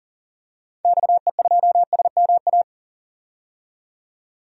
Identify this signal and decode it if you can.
Morse code, heard as a keyed tone, about 38 wpm, 700 Hz. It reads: XE2SMA
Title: DXCC-100 Call Signs - 6 Characters - 40wpm